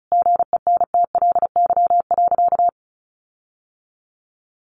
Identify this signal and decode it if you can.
Morse code, heard as a keyed tone, about 34 wpm, 700 Hz. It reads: GENTLY.
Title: Sentences from Top 1500 Words 35wpm